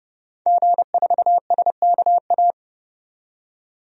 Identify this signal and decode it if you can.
Morse code, heard as a keyed tone, about 30 wpm, 700 Hz. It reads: G4SKA
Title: DXCC-100 Call Signs - 5 Characters - 30wpm